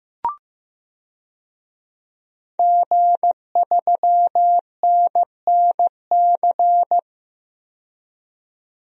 Morse code, heard as a keyed tone, about 16 wpm, 700 Hz. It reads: G3NNC
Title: DXCC-100 Call Signs - 5 Characters - 15wpm